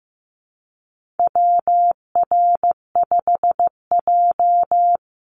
Morse code, heard as a keyed tone, about 15 wpm, 700 Hz. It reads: WR5J
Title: Call Signs Repeated 2x - 15wpm